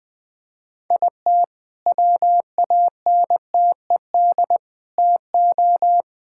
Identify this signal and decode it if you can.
Morse code, heard as a keyed tone, about 21 wpm, 700 Hz. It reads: IT WANTED TO
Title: All-Age Appropriate Jokes - 20wpm